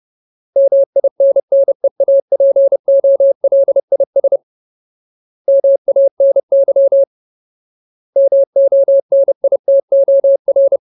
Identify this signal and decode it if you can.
Morse code, heard as a keyed tone, about 30 wpm, 550 Hz. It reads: MINNEAPOLIS MANY MONITOR
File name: Ultimate Word List / 30WPM Ultimate